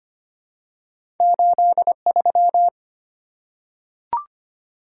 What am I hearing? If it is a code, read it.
Morse code, heard as a keyed tone, about 25 wpm, 700 Hz. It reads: 83
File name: Two Letter-Number Combinations - 25wpm